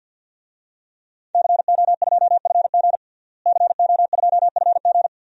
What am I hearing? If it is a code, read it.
Morse code, heard as a keyed tone, about 51 wpm, 700 Hz. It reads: CO2FG CO2FG
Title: DXCC-100 Call Signs - 5 Characters - Repeated 2x - 50wpm